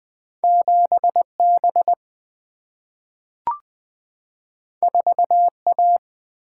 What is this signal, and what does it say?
Morse code, heard as a keyed tone, about 20 wpm, 700 Hz. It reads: 7 B 4 A
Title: Single-Letter Single-Number 20wpm